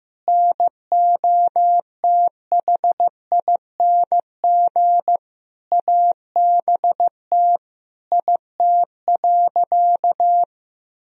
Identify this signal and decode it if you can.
Morse code, heard as a keyed tone, about 15 wpm, 700 Hz. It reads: NOTHING ABT IT.
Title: Sentences from Top 500 Words and Abbreviations 15wpm